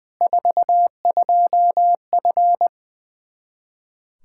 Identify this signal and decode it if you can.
Morse code, heard as a keyed tone, about 20 wpm, 700 Hz. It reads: 42F
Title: Practice QSO - Exchange 2 - Info - 20wpm